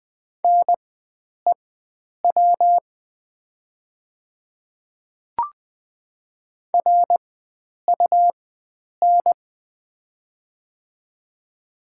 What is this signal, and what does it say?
Morse code, heard as a keyed tone, about 20 wpm, spaced as for 9 wpm, 700 Hz. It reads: NEW RUN
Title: 4x Character Spacing - Top 300 Words - 20wpm